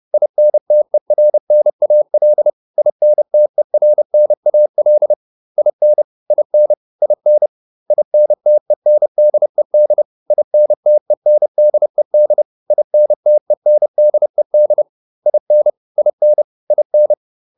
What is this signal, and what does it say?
Morse code, heard as a keyed tone, about 30 wpm, 600 Hz. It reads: INTERNAL INTERNAL IN IN IN INTENDED INTENDED INTENDED IN IN IN